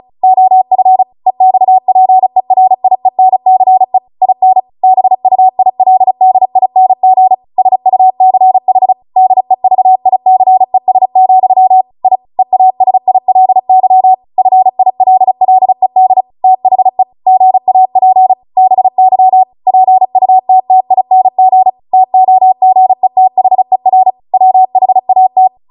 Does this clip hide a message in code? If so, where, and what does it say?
Yes: Morse code from 0.2 to 25.5 s, heard as a keyed tone, about 36 wpm, 750 Hz. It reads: OF EXPERIENCE IN BUILDING SUCH DEVICES, I EASILY FILLED THE GAP BY PUTTING TOGETHER WHAT